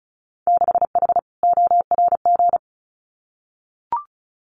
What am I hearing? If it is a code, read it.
Morse code, heard as a keyed tone, about 36 wpm, 700 Hz. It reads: 6H ORG
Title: Winter Field Day Exchanges Encore 35wpm